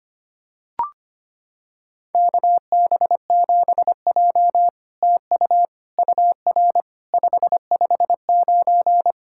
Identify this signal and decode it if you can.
Morse code, heard as a keyed tone, about 25 wpm, 700 Hz. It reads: KB7J TU UR 559
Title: SOTA QSO Practice 25wpm